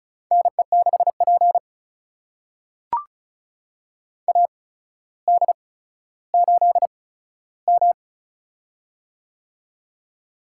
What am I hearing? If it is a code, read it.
Morse code, heard as a keyed tone, about 36 wpm, spaced as for 15 wpm, 700 Hz. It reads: NE6P AD8M
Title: USA Call Signs - 4 Characters - 8x Character Spacing - 35wpm